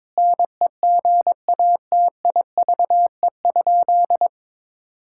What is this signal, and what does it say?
Morse code, heard as a keyed tone, about 21 wpm, 700 Hz. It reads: NEGATIVE?
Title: Sentences from Top 1000 Words - Review Part 6 - 22wpm